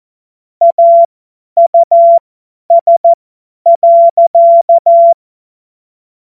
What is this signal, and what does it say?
Morse code, heard as a keyed tone, about 14 wpm, spaced as for 10 wpm, 680 Hz. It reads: AUS.
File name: Morsekurs / Witze / Witz 26. April 2022 50 F70